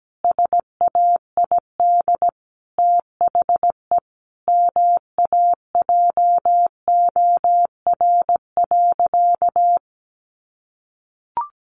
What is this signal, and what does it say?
Morse code, heard as a keyed tone, about 17 wpm, 700 Hz. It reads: SAID THE MAJOR.
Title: Sentences from Top 600 Words 17wpm